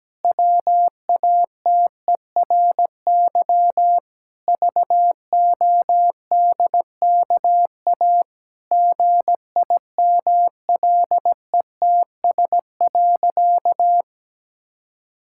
Morse code, heard as a keyed tone, about 17 wpm, 700 Hz. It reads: WATERY VODKA GIMLETS.
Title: Pangram Challenge v2 - 17wpm